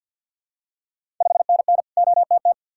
Morse code, heard as a keyed tone, about 52 wpm, 700 Hz. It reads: 5NN OTT
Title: Ontario QSO Party Exchange 50wpm